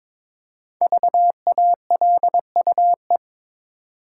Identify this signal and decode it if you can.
Morse code, heard as a keyed tone, about 22 wpm, 690 Hz. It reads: VALUE
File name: 5 Letter Words - First 3 Letters Given - 22wpm